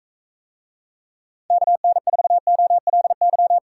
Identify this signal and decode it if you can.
Morse code, heard as a keyed tone, about 45 wpm, 700 Hz. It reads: KN4OLY